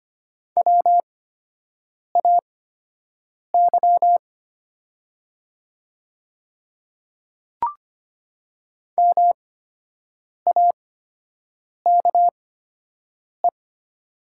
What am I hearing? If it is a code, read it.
Morse code, heard as a keyed tone, about 25 wpm, spaced as for 7 wpm, 700 Hz. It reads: WAY MAKE
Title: 8x Character Spacing - Top 100 Words - 25wpm